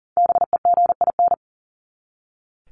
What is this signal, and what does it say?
Morse code, heard as a keyed tone, about 40 wpm, 700 Hz. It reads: BEGIN
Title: Top 300 Words 40wpm